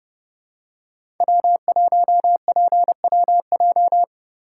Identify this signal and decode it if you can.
Morse code, heard as a keyed tone, about 30 wpm, 700 Hz. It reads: W1PWJ
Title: USA Call Signs - 5 Characters - 2x Character Spacing - 30wpm